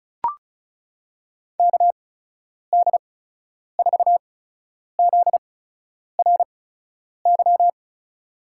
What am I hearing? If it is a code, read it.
Morse code, heard as a keyed tone, about 36 wpm, spaced as for 10 wpm, 700 Hz. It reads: KD4ZRY
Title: USA Call Signs - 6 Characters - 8x Character Spacing - 35wpm